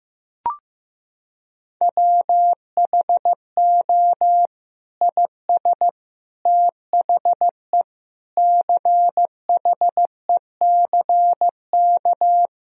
Morse code, heard as a keyed tone, about 15 wpm, 700 Hz. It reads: WHO IS THE CHECK
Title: Sentences from Top 1000 Words - Review Part 3 - 15wpm